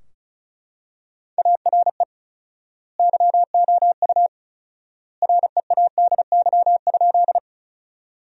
Morse code, heard as a keyed tone, about 34 wpm, 700 Hz. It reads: ARE YOU READY?
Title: 4x Word-Spacing - Sentences from Top 400 Words - 35wpm